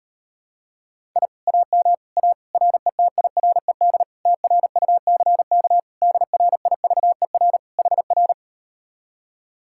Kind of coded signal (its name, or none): Morse code